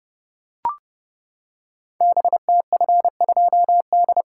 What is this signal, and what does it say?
Morse code, heard as a keyed tone, about 30 wpm, 700 Hz. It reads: BTF2D
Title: Code Groups - 5 Letter:Numbers - 30wpm